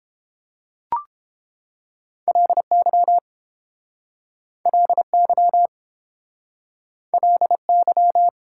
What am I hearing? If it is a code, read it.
Morse code, heard as a keyed tone, about 30 wpm, 700 Hz. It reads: LY LY LY